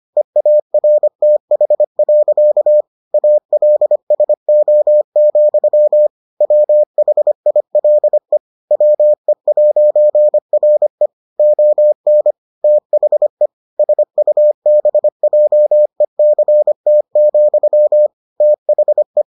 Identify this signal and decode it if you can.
Morse code, heard as a keyed tone, about 25 wpm, 600 Hz. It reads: EARTH. ALSO, WHILE WE'RE ON THE SUBJECT, THE